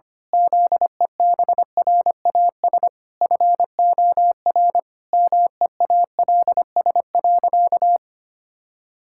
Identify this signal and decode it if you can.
Morse code, heard as a keyed tone, about 25 wpm, 700 Hz. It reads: ZEBRAS FOR MEALS.
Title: Pangram Challenge v2 - 25wpm